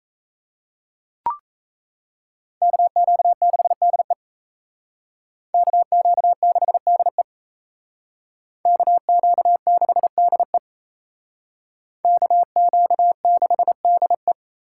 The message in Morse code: KQ6DE KQ6DE KQ6DE KQ6DE

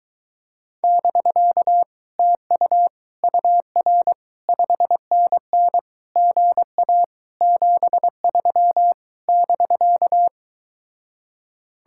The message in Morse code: <BK> TU UR 5NN GA 73 <BK>

